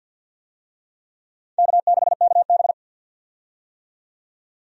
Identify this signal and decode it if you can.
Morse code, heard as a keyed tone, about 53 wpm, 700 Hz. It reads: K6KB
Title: USA Call Signs - 4 Characters - 50wpm